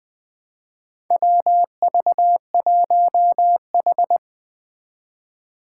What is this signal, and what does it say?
Morse code, heard as a keyed tone, about 20 wpm, 700 Hz. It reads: WV1H
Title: Four Challenging Letter-Number Combinations 20wpm